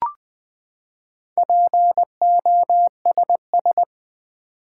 Morse code, heard as a keyed tone, about 20 wpm, 700 Hz. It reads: POSS
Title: Build A Word - 4 Letters At A Time - 10-Letter Words - 20wpm